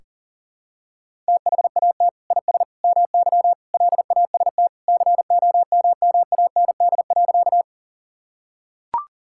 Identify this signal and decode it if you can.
Morse code, heard as a keyed tone, about 40 wpm, 700 Hz. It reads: THAT IS MY LAST COMMAND.
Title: Sentences from Top 1600 Words 40wpm